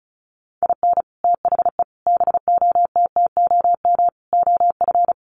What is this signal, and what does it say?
Morse code, heard as a keyed tone, about 34 wpm, 700 Hz. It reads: IN THE BOTTOM OF
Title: Sets of 5 Words Encore 35wpm